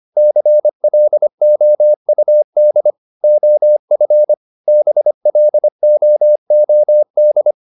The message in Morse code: CLOUD OF BLOOD

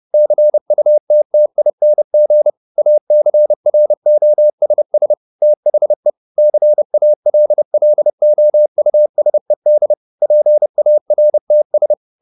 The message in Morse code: CUTTING ACROSS THE CALLOUSED PARTS

